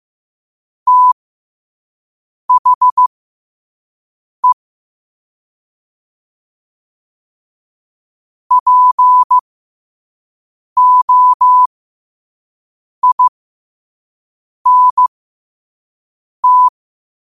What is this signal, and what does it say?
Morse code, heard as a keyed tone, about 15 wpm, spaced as for 5 wpm, 1000 Hz. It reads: THE POINT
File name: qotdfiles / 5wpm